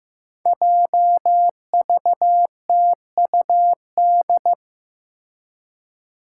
Morse code, heard as a keyed tone, about 15 wpm, 700 Hz. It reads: JVTUD
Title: Five Challenging Letter-Number Combinations 15wpm